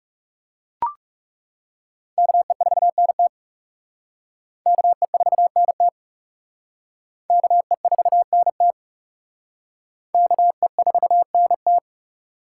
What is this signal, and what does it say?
Morse code, heard as a keyed tone, about 36 wpm, 700 Hz. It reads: KE4NT KE4NT KE4NT KE4NT
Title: Speed Racing - USA 2x2 Call Signs 30wpm